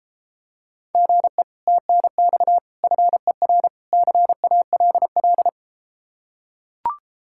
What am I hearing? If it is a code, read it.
Morse code, heard as a keyed tone, about 33 wpm, 700 Hz. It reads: GE TNX FER CALL